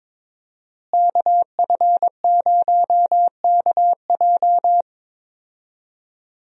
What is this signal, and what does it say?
Morse code, heard as a keyed tone, about 22 wpm, 700 Hz. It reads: KF0KJ
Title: USA Call Signs - 5 Characters - Repeated 2x - 22wpm